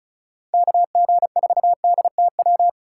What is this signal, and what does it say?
Morse code, heard as a keyed tone, about 36 wpm, 700 Hz. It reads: KG4DTW
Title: USA Call Signs - 6 Characters - 3x Character Spacing - 35wpm